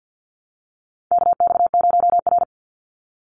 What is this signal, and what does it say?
Morse code, heard as a keyed tone, about 50 wpm, 700 Hz. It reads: KX0R
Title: NA SOTA Activator Callsigns 50wpm